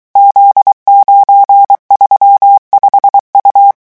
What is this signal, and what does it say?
Morse code, heard as a keyed tone, about 23 wpm, 800 Hz. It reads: Z935U
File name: morse mp3 / revision19-vitesse 22 mots